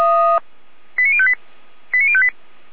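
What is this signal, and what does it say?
Five-tone selective call (ZVEI), heard as quick runs of five tones, tones of about 70 ms. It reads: 89058 89058
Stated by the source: ZVEI five-tone selective call received via SDR, from the feuerwehr-orlamuende fire brigade.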